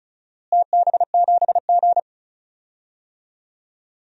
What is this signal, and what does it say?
Morse code, heard as a keyed tone, about 36 wpm, 700 Hz. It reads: TB7G
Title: DXCC-100 Call Signs - 4 Characters - Repeated 2x - 35wpm